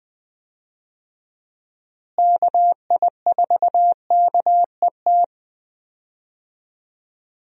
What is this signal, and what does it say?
Morse code, heard as a keyed tone, about 20 wpm, 700 Hz. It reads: KI4KET
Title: USA Call Signs - 6 Characters - Repeated 3x - 20wpm